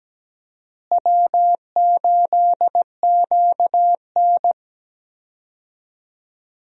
Morse code, heard as a keyed tone, about 17 wpm, 700 Hz. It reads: W8QN
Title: USA Call Signs - 4 Characters - Repeated 3x - 17wpm